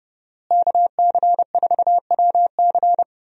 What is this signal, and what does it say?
Morse code, heard as a keyed tone, about 32 wpm, 700 Hz. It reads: KC4WC
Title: USA Call Signs - 5 Characters - Repeated 3x - 30wpm